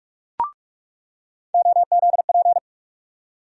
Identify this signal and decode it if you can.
Morse code, heard as a keyed tone, about 41 wpm, 700 Hz. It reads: OZP
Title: USA Call Sign Postfixes - 3 Letters 45wpm